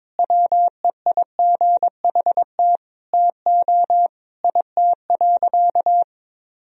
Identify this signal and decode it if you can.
Morse code, heard as a keyed tone, about 22 wpm, 700 Hz. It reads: WEIGHT TO IT.
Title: Sentences from Top 1000 Words - Review Part 3 - 22wpm